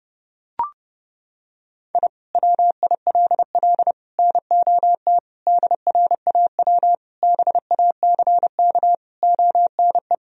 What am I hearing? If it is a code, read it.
Morse code, heard as a keyed tone, about 30 wpm, 700 Hz. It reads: I WILL NOT DRAW BACK ONE
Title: Sentences from Top 1000 Words - Review Part 1 - 30wpm